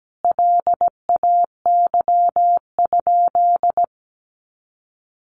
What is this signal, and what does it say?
Morse code, heard as a keyed tone, about 17 wpm, 700 Hz. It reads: LAY?
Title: Sentences from Top 400 Words 17wpm